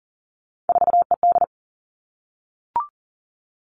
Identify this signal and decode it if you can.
Morse code, heard as a keyed tone, about 41 wpm, 700 Hz. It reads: 4ED
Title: Three Challenging Letter-Number Combinations 40wpm